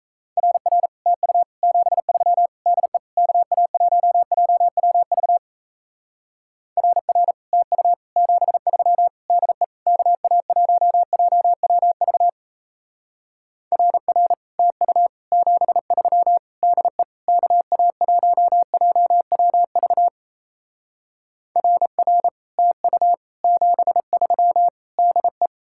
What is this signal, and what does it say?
Morse code, heard as a keyed tone, about 37 wpm, 700 Hz. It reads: RR TU 73 DE KA1JWV RR TU 73 DE KA1JWV RR TU 73 DE KA1JWV RR TU 73 DE